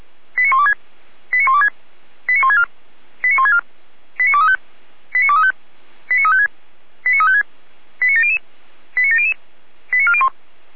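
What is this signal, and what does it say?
Five-tone selective call (ZVEI), heard as quick runs of five tones, tones of about 70 ms. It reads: 89126 89126 89164 89164 89235 89235 89356 89356 89700 89700 89401
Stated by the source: ZVEI five-tone call received by SDR.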